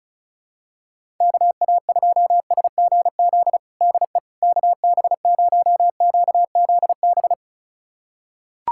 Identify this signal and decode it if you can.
Morse code, heard as a keyed tone, about 36 wpm, 700 Hz. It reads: KA2SGZ DE KB0QZB
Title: USA Callsign DE Callsign 35wpm